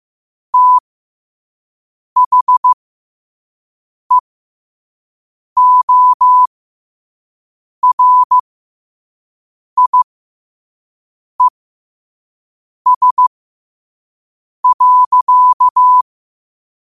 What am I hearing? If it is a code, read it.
Morse code, heard as a keyed tone, about 15 wpm, spaced as for 5 wpm, 1000 Hz. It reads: THEORIES.
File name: 5wpm